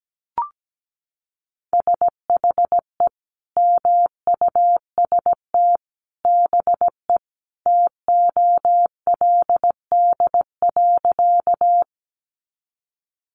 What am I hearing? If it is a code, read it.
Morse code, heard as a keyed tone, about 17 wpm, 700 Hz. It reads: SHE MUST BE TOLD.
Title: Sentences from Top 400 Words 17wpm